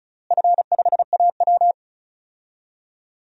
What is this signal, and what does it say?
Morse code, heard as a keyed tone, about 34 wpm, 700 Hz. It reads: F5AW
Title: DXCC-100 Call Signs - 4 Characters - Repeated 3x - 35wpm